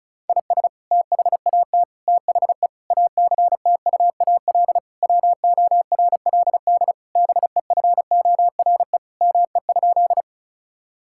Morse code, heard as a keyed tone, about 36 wpm, 700 Hz. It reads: IS THAT THE ACTUAL WORLD BEFORE ME?
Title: Sentences from Top 1000 Words - Review Part 5 - 35wpm